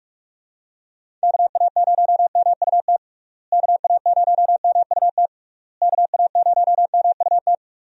Morse code, heard as a keyed tone, about 47 wpm, 700 Hz. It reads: KA0MUT KA0MUT KA0MUT
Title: USA Call Signs - 6 Characters - Repeated 3x - 45wpm